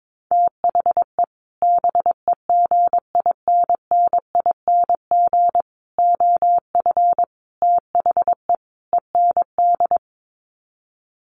Morse code, heard as a keyed tone, about 22 wpm, 700 Hz. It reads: THE BEGINNING OF THE END
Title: Sets of 5 Words Encore2 22wpm